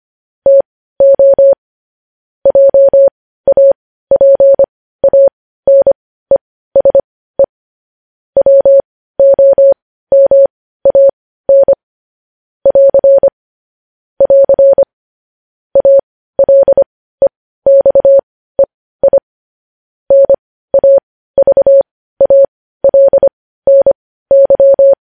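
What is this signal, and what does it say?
Morse code, heard as a keyed tone, about 25 wpm, spaced as for 15 wpm, 550 Hz. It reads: TO JAPANESE WOMAN <AR> <AR> ALEXEI NAVALNY
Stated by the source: Morse code news headlines at 15 wpm.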